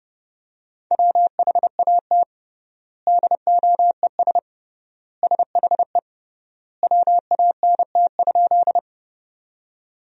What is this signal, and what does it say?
Morse code, heard as a keyed tone, about 32 wpm, 700 Hz. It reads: WHAT DOES SHE WANT?
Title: 3x Word-Spacing - Sentences from Top 200 Words - 30wpm